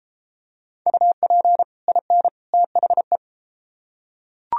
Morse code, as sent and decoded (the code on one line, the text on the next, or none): ..- .--. / .. -. / - .... .
UP IN THE